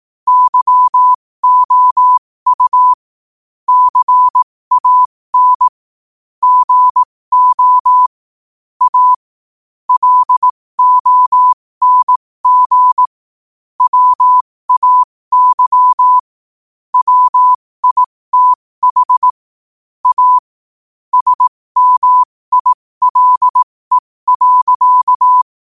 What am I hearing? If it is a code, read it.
Morse code, heard as a keyed tone, about 18 wpm, spaced as for 15 wpm, 1000 Hz. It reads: YOU CAN GO A LONG WAY WITH A SMILE.
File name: qotdfiles / 15wpm